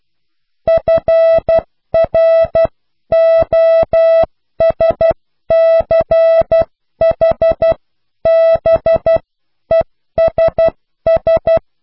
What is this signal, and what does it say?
Morse code, heard as a keyed tone, about 12 wpm, spaced as for 10 wpm, 660 Hz. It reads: FROSCHBESS
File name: cw / pseudo worden(10)